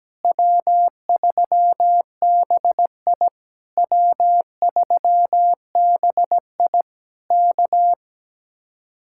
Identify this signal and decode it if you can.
Morse code, heard as a keyed tone, about 17 wpm, 700 Hz. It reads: W3BI W3BI K
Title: Speed-Racing Common QSO Phrases Encore - 17wpm